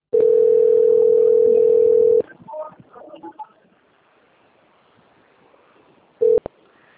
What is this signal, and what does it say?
Ringback tone (North America), 440 + 480 Hz.